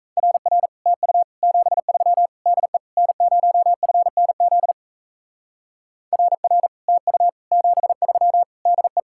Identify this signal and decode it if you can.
Morse code, heard as a keyed tone, about 33 wpm, 700 Hz. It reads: RR TU 73 DE N0FNZ RR TU 73 DE
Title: Speed-Racing Common QSO Phrases - 28wpm